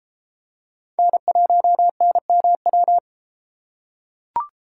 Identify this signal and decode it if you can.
Morse code, heard as a keyed tone, about 31 wpm, 700 Hz. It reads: N1NMW